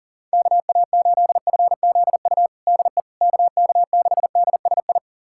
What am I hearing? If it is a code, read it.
Morse code, heard as a keyed tone, about 40 wpm, 700 Hz. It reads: KA8FZU DE KK6DSI
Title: USA Callsign DE Callsign - 6 Characters - 40wpm